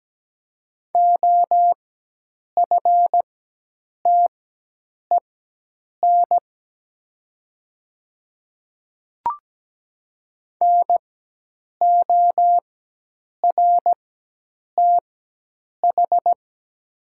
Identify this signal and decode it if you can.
Morse code, heard as a keyed tone, about 17 wpm, spaced as for 8 wpm, 700 Hz. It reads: OFTEN NORTH
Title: 4x Character Spacing - Top 300 Words - 17wpm